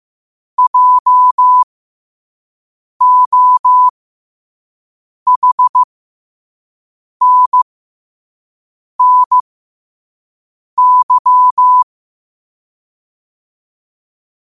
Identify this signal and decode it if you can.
Morse code, heard as a keyed tone, about 15 wpm, spaced as for 5 wpm, 1000 Hz. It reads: JOHNNY
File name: 5wpm